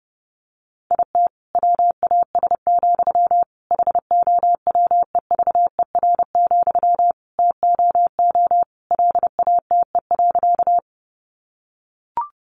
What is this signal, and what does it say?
Morse code, heard as a keyed tone, about 30 wpm, 700 Hz. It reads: IT WAS, HOWEVER, TOO LATE.
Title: Sentences from Top 600 Words 30wpm